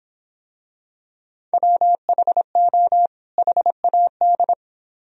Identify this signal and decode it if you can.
Morse code, heard as a keyed tone, about 26 wpm, 700 Hz. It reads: WHO HAD